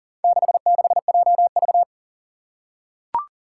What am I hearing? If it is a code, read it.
Morse code, heard as a keyed tone, about 40 wpm, 700 Hz. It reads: 66JV
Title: Four Challenging Letter-Number Combinations 40wpm